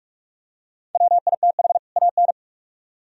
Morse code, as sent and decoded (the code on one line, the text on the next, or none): .-- .. - .... / .- -.
WITH AN